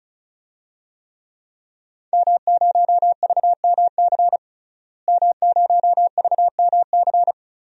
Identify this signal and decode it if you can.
Morse code, heard as a keyed tone, about 34 wpm, 700 Hz. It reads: M0VMC M0VMC
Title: DXCC-100 Call Signs - 5 Characters - Repeated 2x - 35wpm